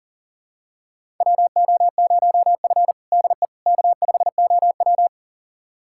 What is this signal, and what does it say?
Morse code, heard as a keyed tone, about 40 wpm, 700 Hz. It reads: WO0F DE K5OW
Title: USA Callsign DE Callsign - 4 Characters - 40wpm